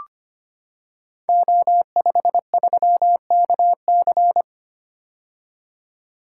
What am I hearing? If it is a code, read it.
Morse code, heard as a keyed tone, about 25 wpm, 700 Hz. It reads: O53KC
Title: Code Groups - 5 Letter:Numbers - 25wpm